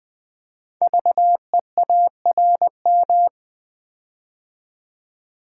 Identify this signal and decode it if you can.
Morse code, heard as a keyed tone, about 20 wpm, 700 Hz. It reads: VEARM
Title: Code Groups - 5 Letters - 20wpm